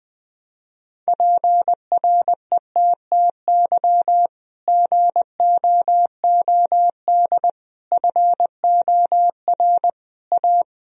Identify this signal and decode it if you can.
Morse code, heard as a keyed tone, about 21 wpm, 700 Hz. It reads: PRETTY GOOD FOR A
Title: Sentences from Top 1000 Words - Review Part 1 - 20wpm